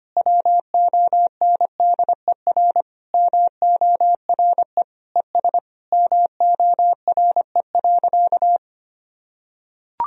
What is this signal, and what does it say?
Morse code, heard as a keyed tone, about 25 wpm, 700 Hz. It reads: WONDER MORE ES MORE.
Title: Sentences from Top 500 Words and Abbreviations 25wpm